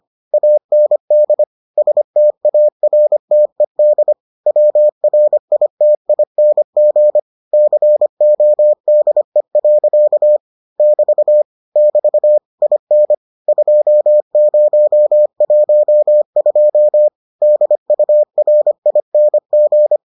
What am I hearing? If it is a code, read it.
Morse code, heard as a keyed tone, about 25 wpm, 600 Hz. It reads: AND STARTED WRITING CODE. <BT> <BT> IN 2012 DURING